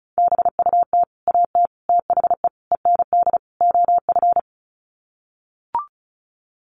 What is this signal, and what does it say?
Morse code, heard as a keyed tone, about 34 wpm, 700 Hz. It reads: BUT AT THE END OF
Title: Sets of 5 Words Encore2 35wpm